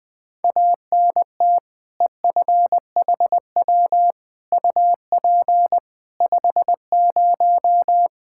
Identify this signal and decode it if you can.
Morse code, heard as a keyed tone, about 20 wpm, 700 Hz. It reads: ANT EFHW UP 50
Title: K1USN SST QSO Practice 20wpm